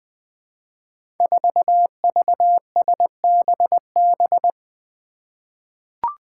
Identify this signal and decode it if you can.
Morse code, heard as a keyed tone, about 20 wpm, 700 Hz. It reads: 4VSBB
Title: Five Challenging Letter-Number Combinations 20wpm